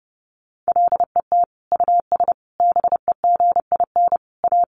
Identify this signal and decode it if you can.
Morse code, heard as a keyed tone, about 30 wpm, 700 Hz. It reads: LET US BEGIN A
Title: Sentences from Top 1400 Words 30wpm